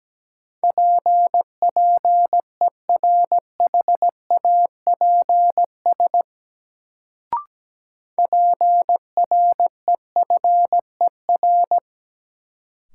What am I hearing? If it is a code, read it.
Morse code, heard as a keyed tone, about 17 wpm, 700 Hz. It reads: P PERHAPS PREFER
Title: Sentences from Top 1100 Words - New Word Drill - 17wpm